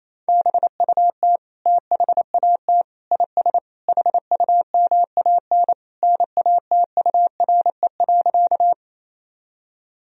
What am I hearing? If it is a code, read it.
Morse code, heard as a keyed tone, about 27 wpm, 700 Hz. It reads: BUT THAT IS HUMAN NATURE.
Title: Sentences from Top 1000 Words - Review Part 3 - 28wpm